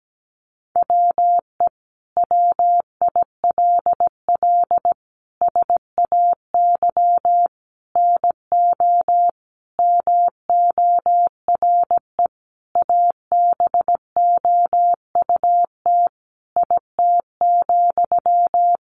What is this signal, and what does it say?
Morse code, heard as a keyed tone, about 17 wpm, 700 Hz. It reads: WE WILL SAY NO MORE ABOUT IT,
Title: Sentences from Top 200 Words 17wpm